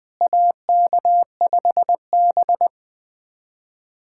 Morse code, heard as a keyed tone, about 20 wpm, 700 Hz. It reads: AK5B
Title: USA Callsign DE Callsign - 4 Characters - 20wpm